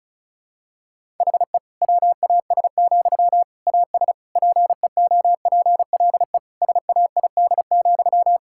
Morse code, heard as a keyed tone, about 36 wpm, 700 Hz. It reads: HE WAS, AS PEOPLE SAID,
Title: Sentences from Top 1000 Words - Review Part 6 - 35wpm